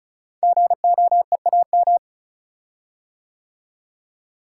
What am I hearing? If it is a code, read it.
Morse code, heard as a keyed tone, about 36 wpm, 700 Hz. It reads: GOEAM
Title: Code Groups - 5 Letters - 35wpm